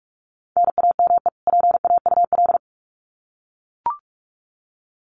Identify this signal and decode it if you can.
Morse code, heard as a keyed tone, about 44 wpm, 700 Hz. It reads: NAME PAUL
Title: Common QSO Phrases 45wpm